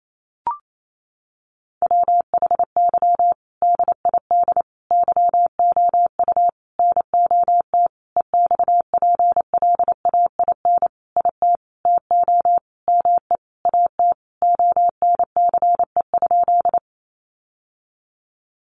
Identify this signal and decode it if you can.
Morse code, heard as a keyed tone, about 28 wpm, 700 Hz. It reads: WHY DID YOU NOT EXPLAIN IT TO ME AT ONCE?
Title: Sentences from Top 1100 Words 28wpm